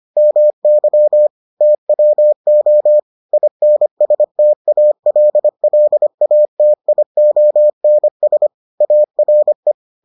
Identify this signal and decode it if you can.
Morse code, heard as a keyed tone, about 25 wpm, 600 Hz. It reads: MY TWO INSTALLATIONS ARE